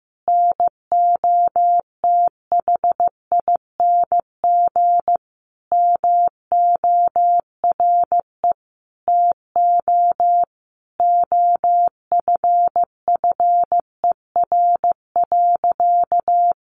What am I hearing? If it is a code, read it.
Morse code, heard as a keyed tone, about 15 wpm, 700 Hz. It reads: NOTHING MORE TO OFFER.
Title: Sentences from Top 1100 Words 15wpm